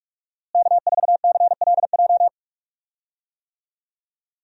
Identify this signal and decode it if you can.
Morse code, heard as a keyed tone, about 46 wpm, 700 Hz. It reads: K4CLJ
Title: USA Call Signs - 5 Characters - Repeated 3x - 45wpm